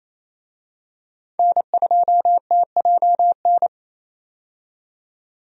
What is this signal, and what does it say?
Morse code, heard as a keyed tone, about 27 wpm, 700 Hz. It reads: N2TJN
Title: USA Call Signs - 5 Characters - 28wpm